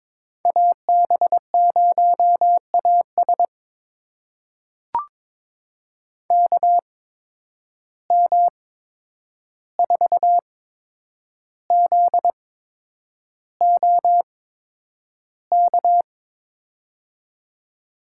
Morse code, heard as a keyed tone, about 22 wpm, spaced as for 9 wpm, 700 Hz. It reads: AB0AS KM4ZOK
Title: USA Call Signs - 8x Character Spacing - 22wpm